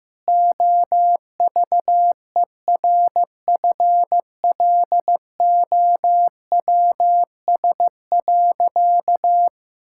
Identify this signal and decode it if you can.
Morse code, heard as a keyed tone, about 15 wpm, 700 Hz. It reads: OVERFLOWS.
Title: Pangram Challenge v2 - 15wpm